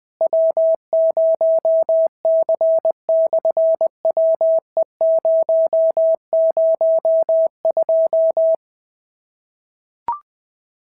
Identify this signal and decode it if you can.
Morse code, heard as a keyed tone, about 20 wpm, 650 Hz. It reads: W0C/WE002
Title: SOTA QSO Practice 20wpm